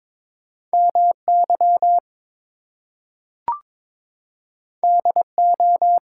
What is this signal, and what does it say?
Morse code, heard as a keyed tone, about 22 wpm, 700 Hz. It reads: MY DO
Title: Two Letter Words - 22wpm